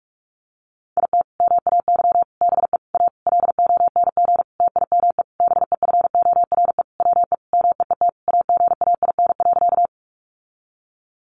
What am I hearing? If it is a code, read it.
Morse code, heard as a keyed tone, about 44 wpm, 700 Hz. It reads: IT MAY BE A LONG TIME BEFORE WE MEET AGAIN.
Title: Sentences from Top 1100 Words 45wpm